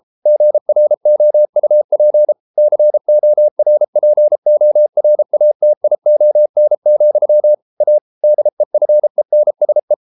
Morse code, heard as a keyed tone, about 33 wpm, 600 Hz. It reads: GROUP CORPORATION, A DEFENSE